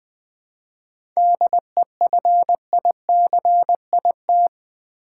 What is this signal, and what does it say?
Morse code, heard as a keyed tone, about 20 wpm, 700 Hz. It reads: DEFICIT
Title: 7 Letter Words - First 3 Letters Given - 20wpm